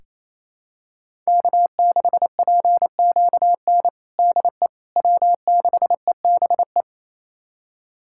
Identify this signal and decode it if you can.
Morse code, heard as a keyed tone, about 27 wpm, 700 Hz. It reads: K6PQN DE W6EBE
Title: USA Callsign DE Callsign - 5 Characters - 28wpm